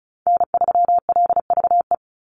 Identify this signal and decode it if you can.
Morse code, heard as a keyed tone, about 34 wpm, 700 Hz. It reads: N3LVE
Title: USA Call Signs 35wpm